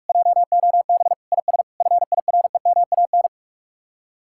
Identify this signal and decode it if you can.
Morse code, heard as a keyed tone, about 45 wpm, 700 Hz. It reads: JOB IS FIREMAN